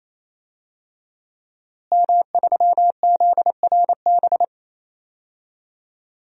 Morse code, heard as a keyed tone, about 28 wpm, 700 Hz. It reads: M3ZRB
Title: DXCC-100 Call Signs - 5 Characters - Repeated 3x - 28wpm